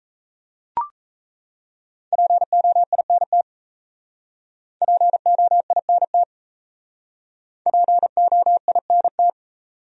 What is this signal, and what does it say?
Morse code, heard as a keyed tone, about 34 wpm, 700 Hz. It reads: POINT POINT POINT
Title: Speed-Racing Top 200 Words 28wpm